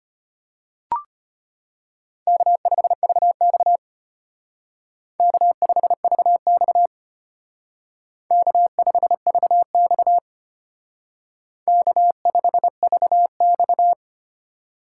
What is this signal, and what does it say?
Morse code, heard as a keyed tone, about 32 wpm, 700 Hz. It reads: K5VX K5VX K5VX K5VX